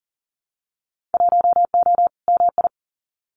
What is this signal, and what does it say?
Morse code, heard as a keyed tone, about 41 wpm, 700 Hz. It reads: 1O MI
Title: Winter Field Day Exchanges Encore 40wpm